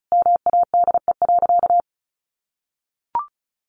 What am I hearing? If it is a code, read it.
Morse code, heard as a keyed tone, about 34 wpm, 700 Hz. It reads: MADE.
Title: Sentences from Top 1200 Words 35wpm